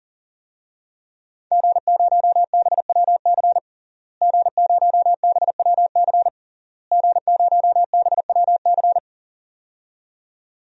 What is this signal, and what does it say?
Morse code, heard as a keyed tone, about 40 wpm, 700 Hz. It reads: G0BWC G0BWC G0BWC
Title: DXCC-100 Call Signs - 5 Characters - Repeated 3x - 40wpm